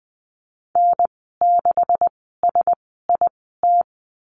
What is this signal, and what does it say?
Morse code, heard as a keyed tone, about 19 wpm, spaced as for 14 wpm, 700 Hz. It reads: N6SIT
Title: 207 Lesson 33g - Letter j - Call Signs 20wpm